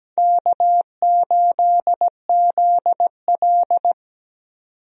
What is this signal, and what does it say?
Morse code, heard as a keyed tone, about 17 wpm, 700 Hz. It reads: K8ZL
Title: USA Call Signs - 4 Characters - 17wpm